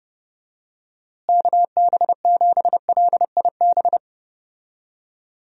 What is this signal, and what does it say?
Morse code, heard as a keyed tone, about 30 wpm, 700 Hz. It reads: KB7LIB
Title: USA Call Signs - 6 Characters - Repeated 2x - 30wpm